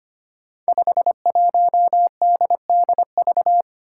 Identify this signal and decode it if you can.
Morse code, heard as a keyed tone, about 25 wpm, 700 Hz. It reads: 51DDV
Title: Five Challenging Letter-Number Combinations 25wpm